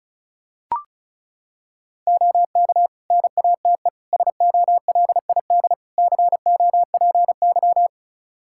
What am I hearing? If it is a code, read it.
Morse code, heard as a keyed tone, about 36 wpm, 700 Hz. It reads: OK NATE SOLID COPY
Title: Practice QSO - Exchange 3 - More Info - 35wpm